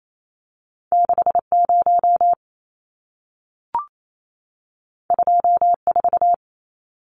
Morse code, heard as a keyed tone, about 28 wpm, 700 Hz. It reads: 60 24
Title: Numbers - 2-Digit 28wpm